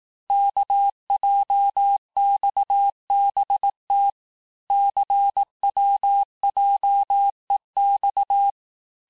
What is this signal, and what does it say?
Morse code, heard as a keyed tone, about 18 wpm, 800 Hz. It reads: KJXBT CWJEX
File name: cours-du-lundi files / cours-1588-texte-chiffre-vitesse-17MPM